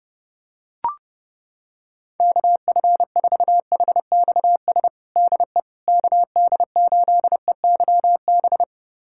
Morse code, heard as a keyed tone, about 32 wpm, 700 Hz. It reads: KF4HXS DE KD8EYB
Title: USA Callsign DE Callsign - 6 Characters - 30wpm